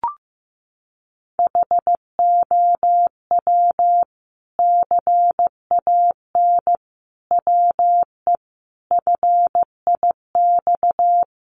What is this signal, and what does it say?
Morse code, heard as a keyed tone, about 15 wpm, 700 Hz. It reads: HOW CAN WE FIX
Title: Sentences from Top 1400 Words 15wpm